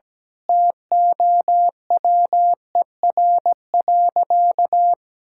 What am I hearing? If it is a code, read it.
Morse code, heard as a keyed tone, about 17 wpm, 700 Hz. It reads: TOWER.
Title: Pangram Challenge v2 - 17wpm